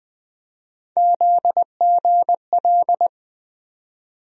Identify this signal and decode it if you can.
Morse code, heard as a keyed tone, about 20 wpm, 700 Hz. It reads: ZGL
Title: USA Call Sign Postfixes - 3 Letters 20wpm